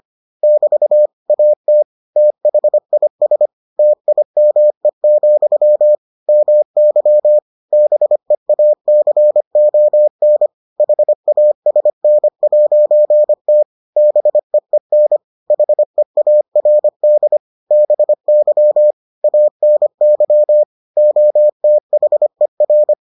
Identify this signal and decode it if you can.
Morse code, heard as a keyed tone, about 25 wpm, 600 Hz. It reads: <BT> AT THIS TIME, MY BEACON HASN'T BEEN HEARD BY ANY OTHER